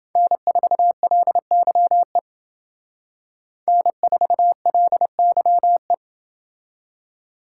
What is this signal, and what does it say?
Morse code, heard as a keyed tone, about 29 wpm, 700 Hz. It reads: N4LYE N4LYE